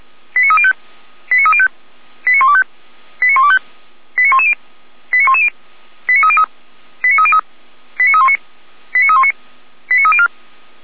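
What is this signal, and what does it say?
Five-tone selective call (ZVEI), heard as quick runs of five tones, tones of about 70 ms. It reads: 89305 89305 89125 89125 89110 89110 89303 89303 89319 89319 89304
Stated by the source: ZVEI five-tone selective call received on SDR, Feuerwehr Orlamünde.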